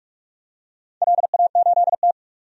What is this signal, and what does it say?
Morse code, heard as a keyed tone, about 46 wpm, 700 Hz. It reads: LA8T
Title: DXCC-100 Call Signs - 4 Characters - 45wpm